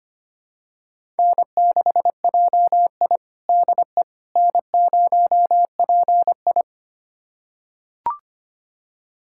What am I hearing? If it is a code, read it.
Morse code, heard as a keyed tone, about 25 wpm, 700 Hz. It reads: N6JI DE N0PI
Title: USA Callsign DE Callsign - 4 Characters - 25wpm